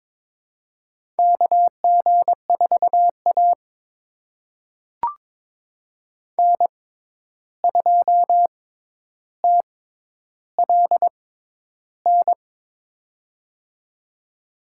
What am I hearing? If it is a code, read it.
Morse code, heard as a keyed tone, about 22 wpm, spaced as for 11 wpm, 700 Hz. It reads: KG4A N2TLN
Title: USA Call Signs - 6x Character Spacing - 22wpm